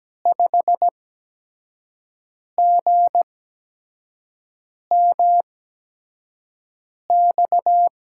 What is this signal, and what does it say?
Morse code, heard as a keyed tone, about 17 wpm, spaced as for 5 wpm, 700 Hz. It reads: 5GMX
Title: USA Call Signs - 6 Characters - 8x Character Spacing - 17wpm